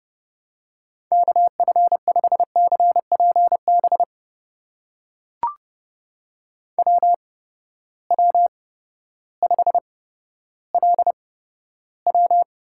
Morse code, heard as a keyed tone, about 31 wpm, spaced as for 14 wpm, 700 Hz. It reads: KF5CPB WW5LW
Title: USA Call Signs - 8x Character Spacing - 30wpm